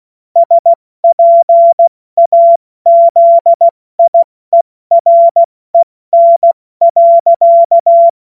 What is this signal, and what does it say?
Morse code, heard as a keyed tone, about 16 wpm, spaced as for 14 wpm, 680 Hz. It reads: SPAZIEREN.
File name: Morsekurs / Witze / Witz 17. August 2021, 70 F80